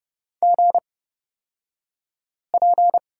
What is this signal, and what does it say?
Morse code, heard as a keyed tone, about 29 wpm, spaced as for 5 wpm, 690 Hz. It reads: GP
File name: Lesson 36d - BT - Two Characters 30wpm